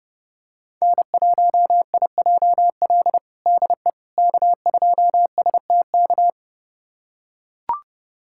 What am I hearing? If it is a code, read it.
Morse code, heard as a keyed tone, about 32 wpm, 700 Hz. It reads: N1IJL DE K2STK
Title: USA Callsign DE Callsign - 5 Characters - 30wpm